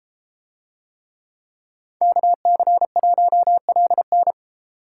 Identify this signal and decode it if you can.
Morse code, heard as a keyed tone, about 32 wpm, 700 Hz. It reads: KC1LN